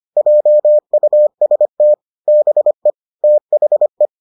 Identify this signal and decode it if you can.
Morse code, heard as a keyed tone, about 25 wpm, 600 Hz. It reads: JUST BE THE